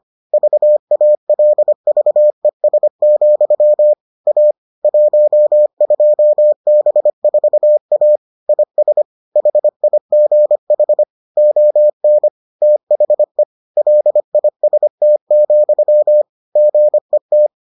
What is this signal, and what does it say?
Morse code, heard as a keyed tone, about 25 wpm, 600 Hz. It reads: VALVES, A 12B4A IS HIGH ON THE LIST, GET